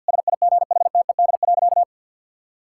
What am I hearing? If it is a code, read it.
Morse code, heard as a keyed tone, about 50 wpm, 700 Hz. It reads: SIGHTED.
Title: Pangram Challenge v2 - 50wpm